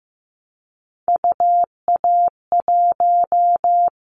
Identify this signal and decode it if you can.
Morse code, heard as a keyed tone, about 15 wpm, 700 Hz. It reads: UA1
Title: Callsign Prefixes for Top 25 DXCC - Repeated 3x - 15wpm